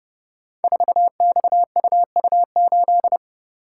Morse code, heard as a keyed tone, about 30 wpm, 700 Hz. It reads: 4XUU8
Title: Code Groups - 5 Letter:Numbers - 30wpm